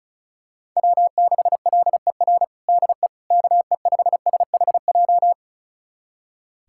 Morse code, heard as a keyed tone, about 36 wpm, 700 Hz. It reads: W6LER DE KE5SHJ
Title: USA Callsign DE Callsign 35wpm